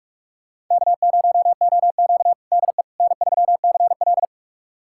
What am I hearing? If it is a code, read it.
Morse code, heard as a keyed tone, about 48 wpm, 700 Hz. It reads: K0OQ DE N3CL
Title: USA Callsign DE Callsign - 4 Characters - 45wpm